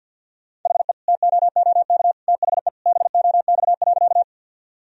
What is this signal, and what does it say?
Morse code, heard as a keyed tone, about 47 wpm, 700 Hz. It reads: HE TOOK THE BOX.
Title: Sentences from Top 1000 Words - Review Part 2 - 50wpm